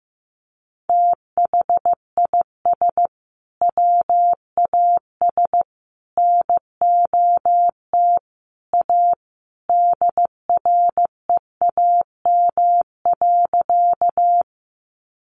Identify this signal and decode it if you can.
Morse code, heard as a keyed tone, about 15 wpm, 700 Hz. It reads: THIS WAS NOT A DREAM.
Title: Sentences from Top 1500 Words 15wpm